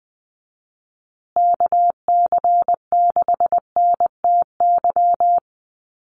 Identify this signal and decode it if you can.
Morse code, heard as a keyed tone, about 20 wpm, 700 Hz. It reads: KC6NTY
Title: USA Call Signs Repeated 2x 20wpm